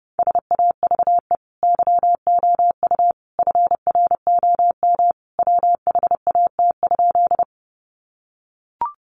Morse code, heard as a keyed tone, about 29 wpm, 700 Hz. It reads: SAVE YOU FROM WHAT?
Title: Sentences from Top 1100 Words 30wpm